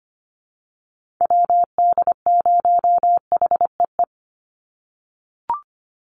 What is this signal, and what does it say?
Morse code, heard as a keyed tone, about 25 wpm, 700 Hz. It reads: WD0HEE
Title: USA Call Signs 25wpm